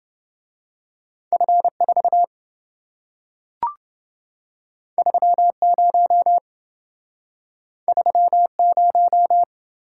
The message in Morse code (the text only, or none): F4 30 30